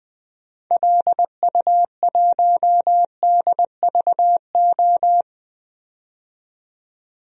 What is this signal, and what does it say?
Morse code, heard as a keyed tone, about 20 wpm, 700 Hz. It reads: LU1DVO
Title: DXCC-100 Call Signs - 6 Characters - 20wpm